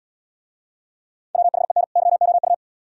Morse code, heard as a keyed tone, about 76 wpm, 700 Hz. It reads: WHEN YOU